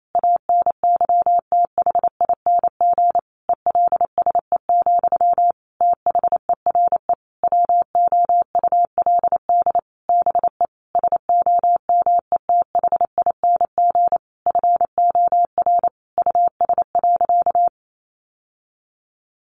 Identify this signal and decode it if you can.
Morse code, heard as a keyed tone, about 28 wpm, 700 Hz. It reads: ANYTHING ELSE, THERE WOULD BE SOMETHING FOR US.
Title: Sentences from Top 1200 Words 28wpm